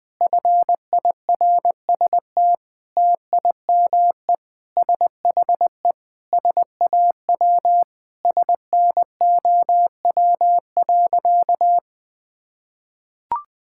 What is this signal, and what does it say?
Morse code, heard as a keyed tone, about 20 wpm, 700 Hz. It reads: FIRST TIME SHE SAW SNOW.
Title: Sentences with 5-Word Phrases - 20wpm